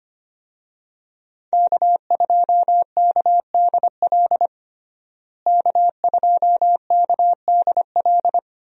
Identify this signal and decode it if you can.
Morse code, heard as a keyed tone, about 25 wpm, 700 Hz. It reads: K2KDL K2KDL
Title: USA Call Signs - 5 Characters - Repeated 2x - 25wpm